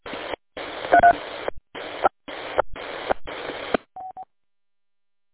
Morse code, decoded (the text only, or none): none